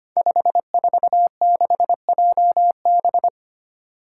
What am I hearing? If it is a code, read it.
Morse code, heard as a keyed tone, about 25 wpm, 700 Hz. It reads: 546JB
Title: Five Challenging Letter-Number Combinations 25wpm